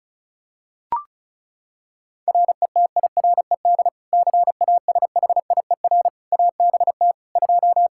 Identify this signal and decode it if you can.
Morse code, heard as a keyed tone, about 34 wpm, 700 Hz. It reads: RETIRED CASHIER ABT 2
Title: K1USN SST QSO Practice 35wpm